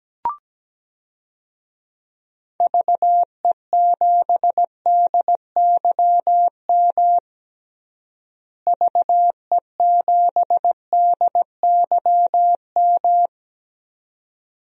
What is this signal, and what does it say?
Morse code, heard as a keyed tone, about 17 wpm, 700 Hz. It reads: VE7DYM VE7DYM
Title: DXCC-100 Call Signs - 6 Characters - Repeated 2x - 17wpm